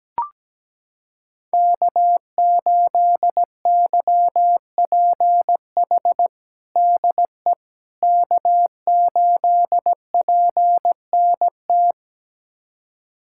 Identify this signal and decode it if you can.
Morse code, heard as a keyed tone, about 17 wpm, 700 Hz. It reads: K8YPH DE K8PNT
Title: USA Callsign DE Callsign - 5 Characters - 17wpm